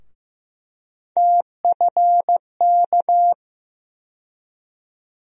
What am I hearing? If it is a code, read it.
Morse code, heard as a keyed tone, about 15 wpm, 700 Hz. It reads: TFK
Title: USA Call Sign Postfixes - 3 Letters 15wpm